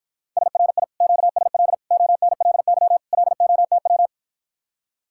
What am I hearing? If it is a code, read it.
Morse code, heard as a keyed tone, about 53 wpm, 700 Hz. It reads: SRI QSL ONLY LOTW